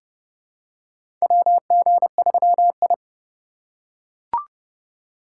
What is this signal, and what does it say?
Morse code, heard as a keyed tone, about 31 wpm, 700 Hz. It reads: WG3I
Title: USA Call Signs - 4 Characters - 2x Character Spacing - 30wpm